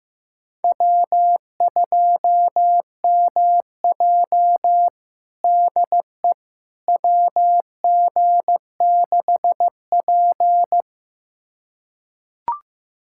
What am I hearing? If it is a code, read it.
Morse code, heard as a keyed tone, about 15 wpm, 700 Hz. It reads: W2MJ DE WG6P
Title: USA Callsign DE Callsign - 4 Characters - 15wpm